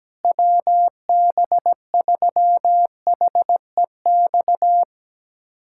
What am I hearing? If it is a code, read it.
Morse code, heard as a keyed tone, about 17 wpm, 700 Hz. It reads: WB3HEX
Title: USA Callsign DE Callsign - 6 Characters - 17wpm